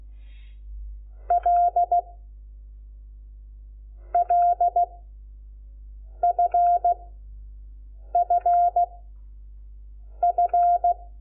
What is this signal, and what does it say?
Morse code, heard as a keyed tone, about 16 wpm, 700 Hz. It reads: L L F F F
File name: Lecciones / CLASE Nº5